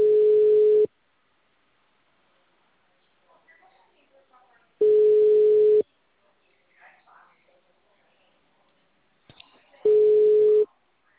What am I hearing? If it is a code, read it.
Ringback tone (Europe, 425 Hz), with speech.
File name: out- -2001- - - .2189